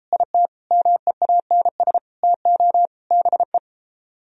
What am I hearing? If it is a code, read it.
Morse code, heard as a keyed tone, about 33 wpm, 700 Hz. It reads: IT MEANS TO BE